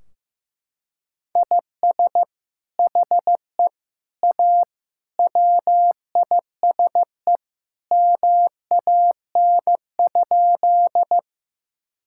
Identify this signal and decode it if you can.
Morse code, heard as a keyed tone, about 16 wpm, 700 Hz. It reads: IS HE A WISE MAN?
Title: Sentences from Top 1600 Words 15wpm